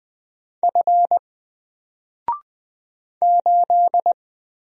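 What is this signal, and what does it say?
Morse code, heard as a keyed tone, about 20 wpm, 700 Hz. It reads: F 8
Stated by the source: Single letter and number Morse drill at 20 wpm, from N4LCD.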